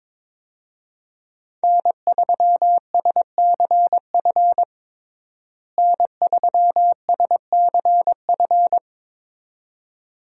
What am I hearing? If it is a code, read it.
Morse code, heard as a keyed tone, about 21 wpm, 700 Hz. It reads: N3SCF N3SCF
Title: USA Call Signs - 5 Characters - Repeated 2x - 22wpm